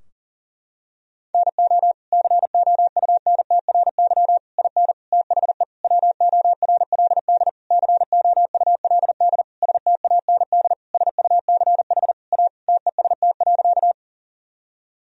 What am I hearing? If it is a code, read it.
Morse code, heard as a keyed tone, about 40 wpm, 700 Hz. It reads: NO COUNTRY IN THE WORLD COULD STAND SUCH A TEST.
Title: Sentences from Top 1000 Words - Review Part 3 - 40wpm